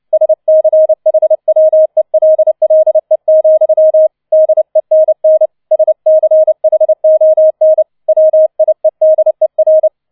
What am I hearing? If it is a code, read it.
Morse code, heard as a keyed tone, about 29 wpm, 630 Hz. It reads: SCHWELLE, DENN SCHON WIEDER